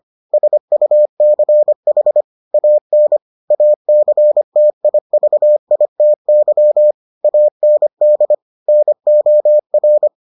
Morse code, heard as a keyed tone, about 25 wpm, 600 Hz. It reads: SUCH AN ACTIVITY AND NOR